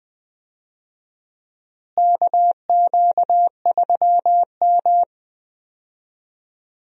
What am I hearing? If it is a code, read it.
Morse code, heard as a keyed tone, about 20 wpm, 700 Hz. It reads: KQ3M
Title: USA Call Signs - 4 Characters - Repeated 3x - 20wpm